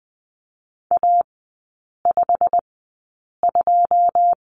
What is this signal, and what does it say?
Morse code, heard as a keyed tone, about 20 wpm, spaced as for 8 wpm, 700 Hz. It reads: A52
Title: 093 Lesson 16e - Letter c - Three Characters 20wpm